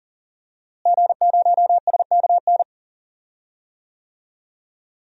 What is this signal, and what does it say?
Morse code, heard as a keyed tone, about 37 wpm, 700 Hz. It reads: G0SKN
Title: DXCC-100 Call Signs - 5 Characters - Repeated 2x - 40wpm